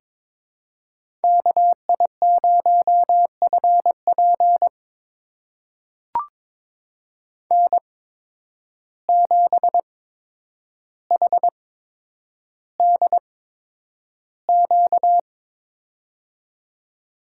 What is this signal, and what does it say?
Morse code, heard as a keyed tone, about 22 wpm, spaced as for 9 wpm, 700 Hz. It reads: KI0FP N7HDQ
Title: USA Call Signs - 8x Character Spacing - 22wpm